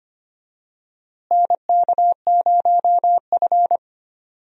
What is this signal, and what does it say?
Morse code, heard as a keyed tone, about 25 wpm, 700 Hz. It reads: NK0F
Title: USA Call Signs - 4 Characters - Repeated 3x - 25wpm